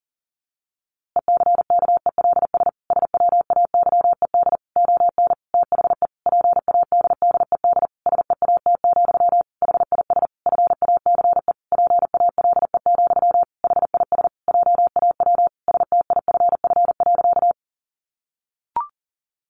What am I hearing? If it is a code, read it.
Morse code, heard as a keyed tone, about 40 wpm, 700 Hz. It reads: ECKELS SWAYED ON THE PADDED SEAT, HIS FACE PALE, HIS JAW STIFF.
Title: A Sound Of Thunder 40wpm